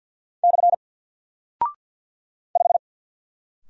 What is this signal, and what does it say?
Morse code, heard as a keyed tone, about 50 wpm, 700 Hz. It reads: / 5
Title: Single Character (Letter:Number:Punctuation:Prosign) 50wpm